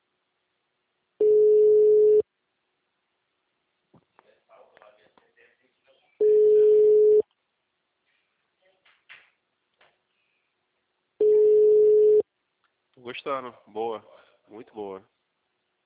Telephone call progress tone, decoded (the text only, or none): ringback tone